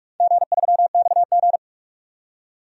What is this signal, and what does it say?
Morse code, heard as a keyed tone, about 46 wpm, 700 Hz. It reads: G3XG
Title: DXCC-100 Call Signs - 4 Characters - 45wpm